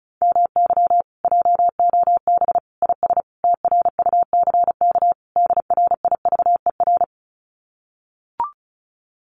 Morse code, heard as a keyed tone, about 36 wpm, 700 Hz. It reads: MY JOB IS TRUCK DRIVER
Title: Common QSO Phrases Encore 35wpm